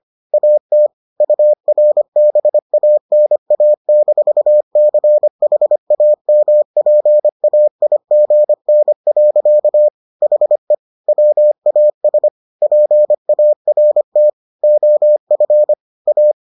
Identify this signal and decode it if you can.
Morse code, heard as a keyed tone, about 25 wpm, 600 Hz. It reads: AT URBANA-CHAMPAIGN. HE WAS PART OF A